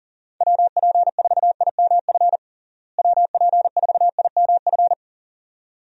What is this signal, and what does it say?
Morse code, heard as a keyed tone, about 40 wpm, 700 Hz. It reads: WP4IMF WP4IMF
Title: USA Call Signs - 6 Characters - Repeated 2x - 40wpm